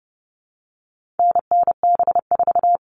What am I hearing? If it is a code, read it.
Morse code, heard as a keyed tone, about 30 wpm, 700 Hz. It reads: NNB4
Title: All Cut Numbers - 4-Digit 30wpm